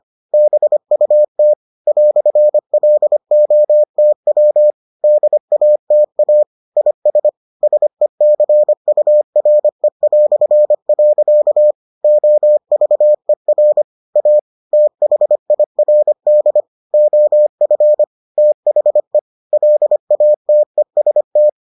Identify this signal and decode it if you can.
Morse code, heard as a keyed tone, about 25 wpm, 600 Hz. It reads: BUT "LOTW DATA IS SECURE". OVER A THIRD OF THE LATEST